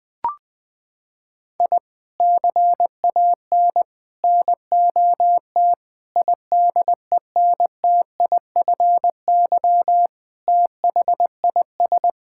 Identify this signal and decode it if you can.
Morse code, heard as a keyed tone, about 20 wpm, 700 Hz. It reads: I CAN NOT IDENTIFY THIS
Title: Sentences from Top 1000 Words - Review Part 6 - 20wpm